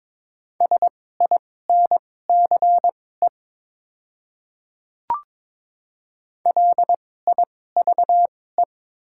Morse code, heard as a keyed tone, about 22 wpm, spaced as for 16 wpm, 700 Hz. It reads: SINCE LIVE
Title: 2x Character Spacing - Top 300 Words - 22wpm